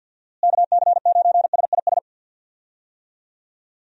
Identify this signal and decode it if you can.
Morse code, heard as a keyed tone, about 46 wpm, 700 Hz. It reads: KC9SIS
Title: USA Call Signs - 6 Characters - Repeated 2x - 50wpm